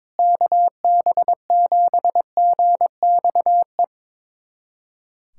Morse code, heard as a keyed tone, about 22 wpm, 700 Hz. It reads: KB7GXE
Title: USA Callsign DE Callsign - 6 Characters - 22wpm